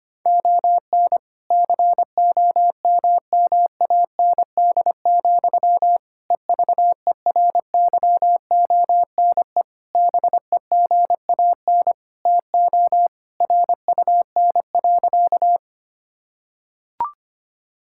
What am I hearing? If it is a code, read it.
Morse code, heard as a keyed tone, about 25 wpm, 700 Hz. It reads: ON COMMAND, EVERYONE BEGAN TO RUN.
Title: Sentences from Top 1600 Words 25wpm